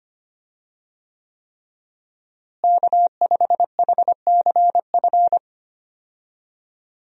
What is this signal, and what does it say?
Morse code, heard as a keyed tone, about 25 wpm, 700 Hz. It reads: K5HCF
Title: USA Call Signs - 5 Characters - Repeated 3x - 25wpm